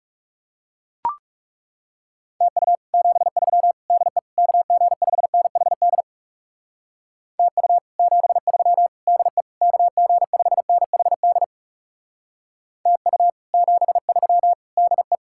Morse code, heard as a keyed tone, about 38 wpm, 700 Hz. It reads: TU 73 DE KG5NHD TU 73 DE KG5NHD TU 73 DE
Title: Speed-Racing Common QSO Phrases - 30wpm